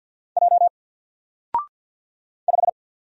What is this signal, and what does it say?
Morse code, heard as a keyed tone, about 50 wpm, 700 Hz. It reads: J 5
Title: Single Letter-Number 50wpm